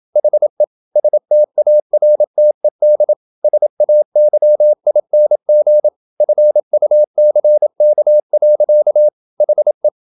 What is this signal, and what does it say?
Morse code, heard as a keyed tone, about 27 wpm, 600 Hz. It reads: HE STARTED SAYING FUCK. HE
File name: Never 0027wpm